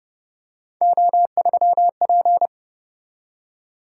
Morse code, heard as a keyed tone, about 30 wpm, 700 Hz. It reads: O3P